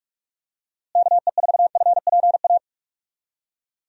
Morse code, heard as a keyed tone, about 41 wpm, 700 Hz. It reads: KE4FPA
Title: USA Call Signs - 6 Characters - Repeated 3x - 45wpm